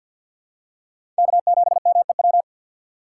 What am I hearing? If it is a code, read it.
Morse code, heard as a keyed tone, about 53 wpm, 700 Hz. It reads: K7MEW
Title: USA Call Signs - 2x Character Spacing - 50wpm